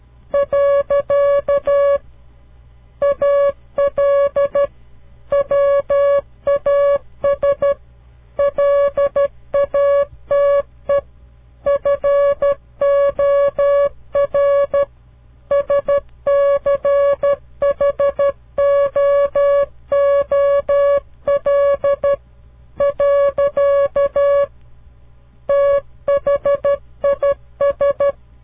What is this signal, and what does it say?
Morse code, heard as a keyed tone, about 13 wpm, 560 Hz. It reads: . AL WAS LATE FOR SCHOOL . THIS
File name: CW Learning Programs / farnsworth / unit 2